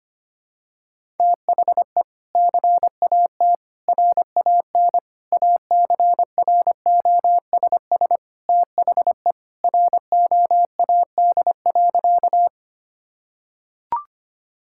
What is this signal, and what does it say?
Morse code, heard as a keyed tone, about 25 wpm, 700 Hz. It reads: THE CAT RAN ACROSS THE ROAD.
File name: practice / 25wpm / Sentences with 2-Word Phrases - 25wpm